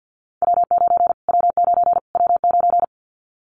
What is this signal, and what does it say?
Morse code, heard as a keyed tone, about 50 wpm, 700 Hz. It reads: W9 W9 W9
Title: Callsign Prefixes for Top 25 DXCC - Repeated 3x - 50wpm